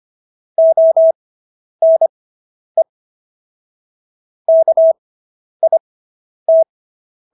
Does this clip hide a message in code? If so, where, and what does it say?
Yes: Morse code from 0.6 to 6.6 s, heard as a keyed tone, about 25 wpm, spaced as for 10 wpm, 650 Hz. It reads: ONE KIT